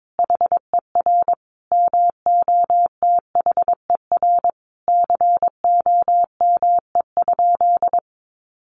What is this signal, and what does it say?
Morse code, heard as a keyed tone, about 22 wpm, 700 Hz. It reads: HER MOTHER COME?
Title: Sentences from Top 200 Words 22wpm